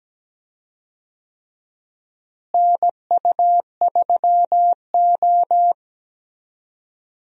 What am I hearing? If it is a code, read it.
Morse code, heard as a keyed tone, about 17 wpm, 700 Hz. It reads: NU3O
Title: USA Call Signs - 4 Characters - Repeated 3x - 17wpm